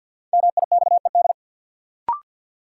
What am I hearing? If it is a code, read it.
Morse code, heard as a keyed tone, about 47 wpm, 700 Hz. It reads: MIXED
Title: Sentences from Top 1500 Words - New Word Drill Alt - 50wpm